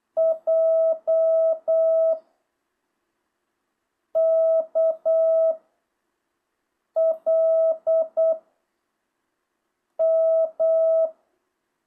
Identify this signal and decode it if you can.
Morse code, heard as a keyed tone, about 8 wpm, 660 Hz. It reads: J K L M
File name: jh8cbh